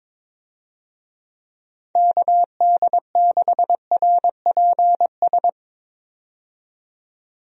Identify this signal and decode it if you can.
Morse code, heard as a keyed tone, about 22 wpm, 700 Hz. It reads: KD6RPS
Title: USA Call Signs - 6 Characters - 22wpm